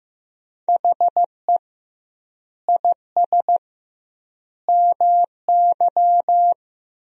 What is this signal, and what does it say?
Morse code, heard as a keyed tone, about 15 wpm, 700 Hz. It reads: HE IS MY
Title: 2x Word-Spacing - Sentences from Top 300 Words - 15wpm